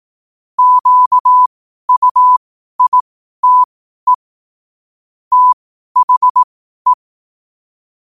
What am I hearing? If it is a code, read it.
Morse code, heard as a keyed tone, about 18 wpm, spaced as for 12 wpm, 1000 Hz. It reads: QUITE THE
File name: qotdfiles / 12wpm